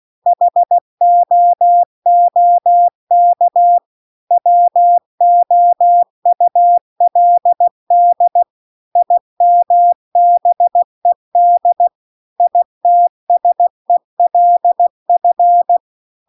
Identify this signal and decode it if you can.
Morse code, heard as a keyed tone, about 16 wpm, 700 Hz. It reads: HOOK WOULD IMBED ITSELF
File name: CW-The-Big-Twoheaded-River-2-16WPM